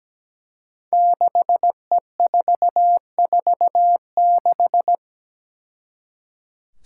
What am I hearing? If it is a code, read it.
Morse code, heard as a keyed tone, about 17 wpm, 700 Hz. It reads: 6E446
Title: Five Challenging Letter-Number Combinations 17wpm